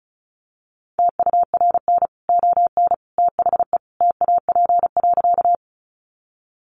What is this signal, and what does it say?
Morse code, heard as a keyed tone, about 34 wpm, 700 Hz. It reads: TURN ON THE TAP.
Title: Sentences from Top 1400 Words 35wpm